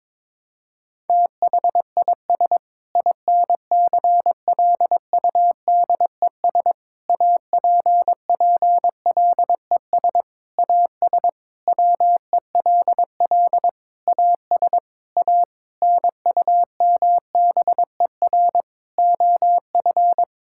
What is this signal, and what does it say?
Morse code, heard as a keyed tone, about 22 wpm, 700 Hz. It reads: THIS INCLUDES APPLES AS WELL AS A NUMBER OF
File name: Sentences with 7-Word Phrases - 22wpm